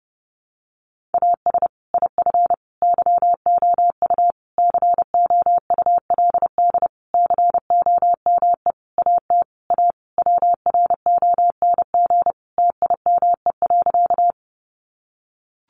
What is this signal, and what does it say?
Morse code, heard as a keyed tone, about 30 wpm, 700 Hz. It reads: AS IF YOU COULD COME AT A WRONG TIME.
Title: Sentences from Top 900 Words 30wpm